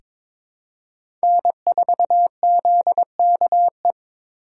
Morse code, heard as a keyed tone, about 22 wpm, 700 Hz. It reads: N4ZKE
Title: USA Call Signs - 5 Characters - 8x Character Spacing - 22wpm